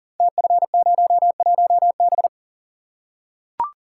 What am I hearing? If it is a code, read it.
Morse code, heard as a keyed tone, about 39 wpm, 700 Hz. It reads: TF01B
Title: Code Groups - 5 Letter:Numbers - 40wpm